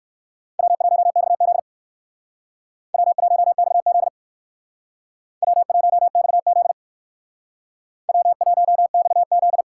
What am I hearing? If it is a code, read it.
Morse code, heard as a keyed tone, about 58 wpm, 700 Hz. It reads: W1XZ W1XZ W1XZ W1XZ